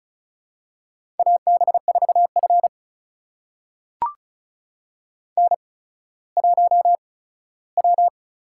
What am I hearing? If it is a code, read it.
Morse code, heard as a keyed tone, about 36 wpm, spaced as for 17 wpm, 700 Hz. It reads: AB4F N1W
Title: USA Call Signs - 8x Character Spacing - 35wpm